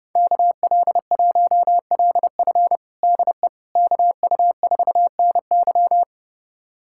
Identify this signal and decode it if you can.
Morse code, heard as a keyed tone, about 30 wpm, 700 Hz. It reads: KL1LF DE KU4NY
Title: USA Callsign DE Callsign - 5 Characters - 30wpm